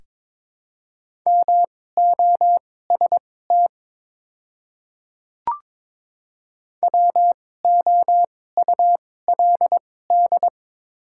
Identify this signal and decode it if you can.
Morse code, heard as a keyed tone, about 22 wpm, spaced as for 16 wpm, 700 Hz. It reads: MOST WOULD
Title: 2x Character Spacing - Top 500 Words - 22wpm